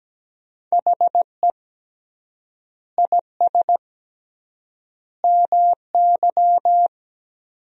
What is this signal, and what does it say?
Morse code, heard as a keyed tone, about 17 wpm, 700 Hz. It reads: HE IS MY
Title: 3x Word-Spacing - Sentences from Top 300 Words - 17wpm